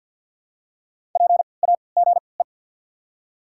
Morse code, heard as a keyed tone, about 48 wpm, spaced as for 28 wpm, 700 Hz. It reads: PAGE
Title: 3x Character Spacing - Top 500 Words - 50wpm